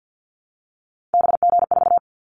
Morse code, heard as a keyed tone, about 50 wpm, 700 Hz. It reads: BG4
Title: Callsign Prefixes for Top 100 DXCC - 50wpm